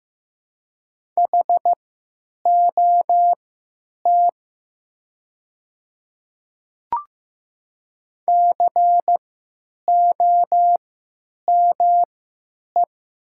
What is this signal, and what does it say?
Morse code, heard as a keyed tone, about 15 wpm, spaced as for 9 wpm, 700 Hz. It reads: HOT COME
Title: 3x Character Spacing - Top 100 Words - 15wpm